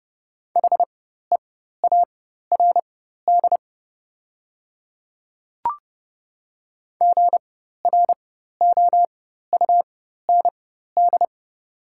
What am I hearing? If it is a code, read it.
Morse code, heard as a keyed tone, about 30 wpm, spaced as for 14 wpm, 700 Hz. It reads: HEARD GROUND
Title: 4x Character Spacing - Top 500 Words - 30wpm